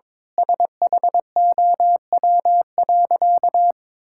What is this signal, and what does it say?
Morse code, heard as a keyed tone, about 21 wpm, 700 Hz. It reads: SHOW.
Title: Sentences from Top 1000 Words - Review Part 6 - 22wpm